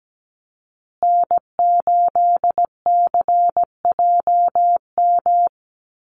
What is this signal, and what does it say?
Morse code, heard as a keyed tone, about 17 wpm, 700 Hz. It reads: N8CJM
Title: USA Call Signs Repeated 3x 17wpm